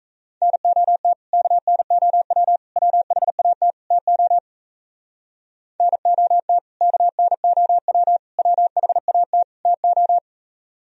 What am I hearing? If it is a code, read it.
Morse code, heard as a keyed tone, about 34 wpm, 700 Hz. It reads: NOT KNOW WHAT TO NOT KNOW WHAT TO
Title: Speed-Racing - Sets of 4 Words 28wpm